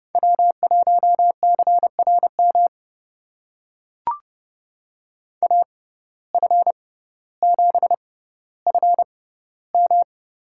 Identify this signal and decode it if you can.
Morse code, heard as a keyed tone, about 31 wpm, spaced as for 15 wpm, 700 Hz. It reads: W1CRM AF7FM
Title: USA Call Signs - 5 Characters - 6x Character Spacing - 30wpm